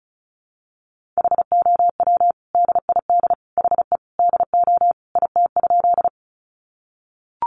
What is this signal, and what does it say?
Morse code, heard as a keyed tone, about 34 wpm, 700 Hz. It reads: HOW DID HE DO IT?
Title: Sentences from Top 100 Words 35wpm